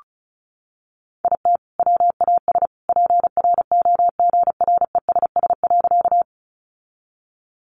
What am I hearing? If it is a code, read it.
Morse code, heard as a keyed tone, about 34 wpm, 700 Hz. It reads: IT WAS PROGRESS.
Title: Sentences from Top 1400 Words 35wpm